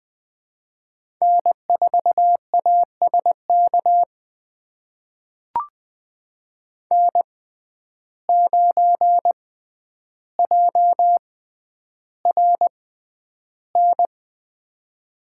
Morse code, heard as a keyed tone, about 20 wpm, spaced as for 10 wpm, 700 Hz. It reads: N4ASK N9JRN
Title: USA Call Signs - 5 Characters - 6x Character Spacing - 20wpm